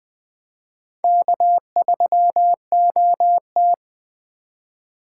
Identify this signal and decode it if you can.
Morse code, heard as a keyed tone, about 20 wpm, 700 Hz. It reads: K3OT
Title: USA Call Signs - 2x Character Spacing - 20wpm